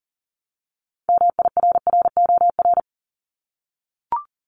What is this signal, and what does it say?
Morse code, heard as a keyed tone, about 40 wpm, 700 Hz. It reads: MIRROR
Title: Multi-letter Phonemes 40wpm